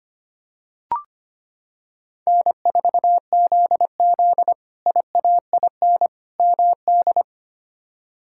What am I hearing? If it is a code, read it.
Morse code, heard as a keyed tone, about 25 wpm, 700 Hz. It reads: N4ZZ IAIN MD
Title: K1USN SST QSO Practice - 25wpm